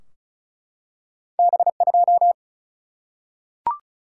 Morse code, heard as a keyed tone, about 36 wpm, 700 Hz. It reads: B 2
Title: Single-Letter Single-Number 35wpm